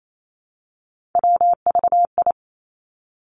Morse code, heard as a keyed tone, about 28 wpm, 700 Hz. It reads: WVI
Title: Three Challenging Letter-Number Combinations 28wpm